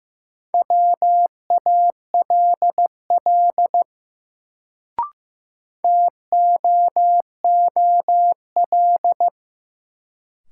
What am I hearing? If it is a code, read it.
Morse code, heard as a keyed tone, about 15 wpm, 700 Hz. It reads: WALL TOOL
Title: Sentences from Top 1000 Words - Review Part 6 - New Word Drill - 15wpm